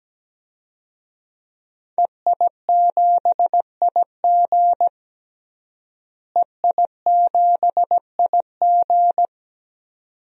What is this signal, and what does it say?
Morse code, heard as a keyed tone, about 17 wpm, 700 Hz. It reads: EI7IG EI7IG
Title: DXCC-100 Call Signs - 5 Characters - Repeated 2x - 17wpm